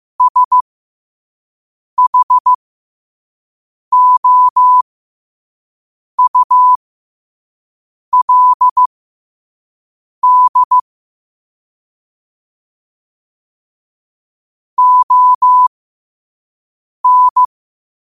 Morse code, heard as a keyed tone, about 15 wpm, spaced as for 5 wpm, 1000 Hz. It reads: SHOULD ON
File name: qotdfiles / 5wpm